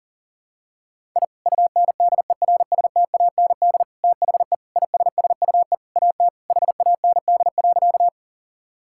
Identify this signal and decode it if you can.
Morse code, heard as a keyed tone, about 40 wpm, 700 Hz. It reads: I UNDERSTAND THE ISSUE AT HAND.
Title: Sentences from Top 1000 Words - Review Part 6 - 40wpm